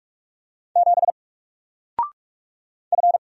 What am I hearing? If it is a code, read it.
Morse code, heard as a keyed tone, about 44 wpm, 700 Hz. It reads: 7 F
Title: Single Letter-Number 45wpm